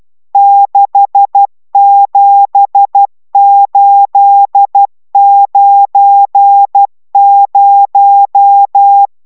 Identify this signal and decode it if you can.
Morse code, heard as a keyed tone, about 12 wpm, 800 Hz. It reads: 67890